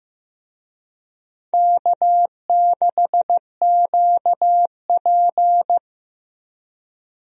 Morse code, heard as a keyed tone, about 15 wpm, 700 Hz. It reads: K6QP
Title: USA Call Signs - 4 Characters - Repeated 2x - 15wpm